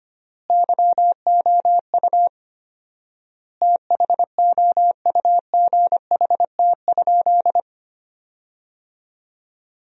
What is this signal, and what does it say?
Morse code, heard as a keyed tone, about 25 wpm, 700 Hz. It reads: YOU THOUGHT?
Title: 4x Word-Spacing - Sentences from Top 300 Words - 25wpm